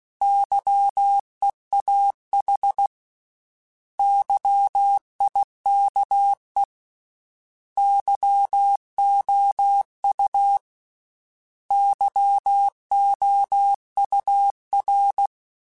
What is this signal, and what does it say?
Morse code, heard as a keyed tone, about 16 wpm, 800 Hz. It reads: YEAH YIKE YOU YOUR
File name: Misc / Misc-mp3s / Misc @15